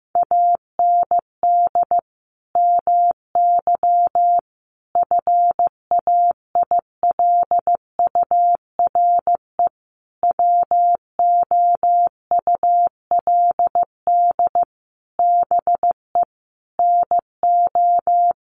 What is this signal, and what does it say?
Morse code, heard as a keyed tone, about 15 wpm, 700 Hz. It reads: AND MY FAILURE WOULD BE NO
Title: Sentences from Top 1400 Words 15wpm